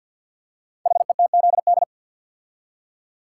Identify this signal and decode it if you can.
Morse code, heard as a keyed tone, about 48 wpm, 700 Hz. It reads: HETZD
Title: Code Groups - 5 Letters - 50wpm